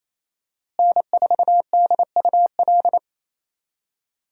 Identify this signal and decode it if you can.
Morse code, heard as a keyed tone, about 28 wpm, 700 Hz. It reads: N4DUL
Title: USA Call Signs - 5 Characters - Repeated 3x - 28wpm